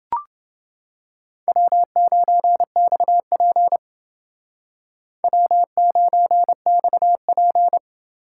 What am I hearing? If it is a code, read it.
Morse code, heard as a keyed tone, about 29 wpm, 700 Hz. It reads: W9XP W9XP